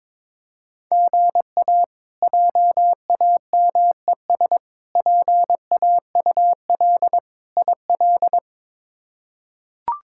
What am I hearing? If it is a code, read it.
Morse code, heard as a keyed tone, about 22 wpm, 700 Hz. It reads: GA JAMES PAUL IL
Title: K1USN SST QSO Practice - 22wpm